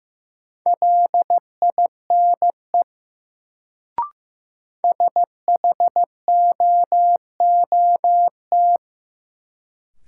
Morse code, heard as a keyed tone, about 15 wpm, 700 Hz. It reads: LINE SHOOT
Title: Sentences from Top 1200 Words - New Word Drill Alt - 15wpm